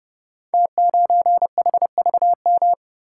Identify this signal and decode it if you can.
Morse code, heard as a keyed tone, about 30 wpm, 700 Hz. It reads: T9HVM
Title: Code Groups - 5 Letter:Numbers - 30wpm